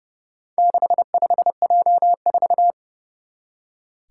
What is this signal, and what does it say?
Morse code, heard as a keyed tone, about 30 wpm, 700 Hz. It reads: 65J4
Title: Four Challenging Letter-Number Combinations 30wpm